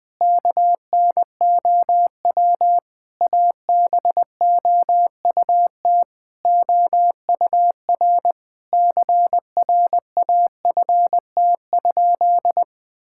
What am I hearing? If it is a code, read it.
Morse code, heard as a keyed tone, about 20 wpm, 700 Hz. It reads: KNOW ABOUT OUR CRAFT?
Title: Sentences from Top 1000 Words - Review Part 6 - 20wpm